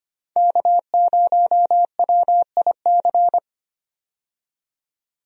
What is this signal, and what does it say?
Morse code, heard as a keyed tone, about 25 wpm, 700 Hz. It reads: K0WIC
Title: USA Call Signs - 5 Characters - 25wpm